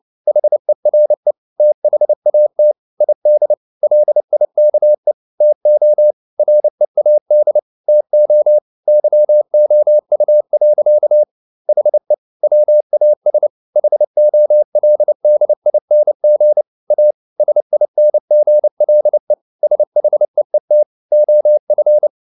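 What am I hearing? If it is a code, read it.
Morse code, heard as a keyed tone, about 29 wpm, 600 Hz. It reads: HERE THAT ID LIKE TO READ TO YOU. HE WAS HOLDING A SINGLE SHEET OF
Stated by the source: Morse code at 29 wpm, sending Ken Follett's novel Never in English, from CW med Gnister.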